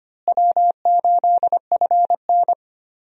Morse code, heard as a keyed tone, about 25 wpm, 700 Hz. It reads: W8FN
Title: USA Call Signs - 4 Characters - Repeated 3x - 25wpm